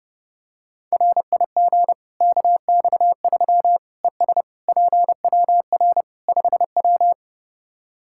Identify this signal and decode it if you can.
Morse code, heard as a keyed tone, about 30 wpm, 700 Hz. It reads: RIG KX3 ES PWR 5W
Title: Practice QSO - Exchange 2 - Info - 30wpm